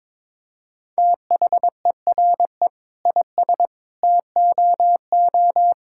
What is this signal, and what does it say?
Morse code, heard as a keyed tone, about 22 wpm, 700 Hz. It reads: THERE IS TOO
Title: Sentences from Top 1000 Words - Review Part 5 - 22wpm